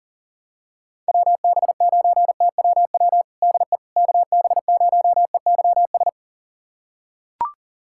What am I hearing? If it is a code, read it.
Morse code, heard as a keyed tone, about 40 wpm, 700 Hz. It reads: WB9TWW DE KB0EYS
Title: USA Callsign DE Callsign - 6 Characters - 40wpm